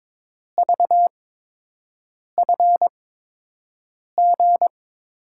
Morse code, heard as a keyed tone, about 22 wpm, spaced as for 6 wpm, 700 Hz. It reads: VFG
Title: USA Call Signs - 6 Characters - 8x Character Spacing - 22wpm